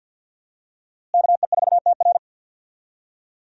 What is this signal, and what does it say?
Morse code, heard as a keyed tone, about 52 wpm, 700 Hz. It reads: KE4TR
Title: USA Call Signs - 5 Characters - Repeated 3x - 50wpm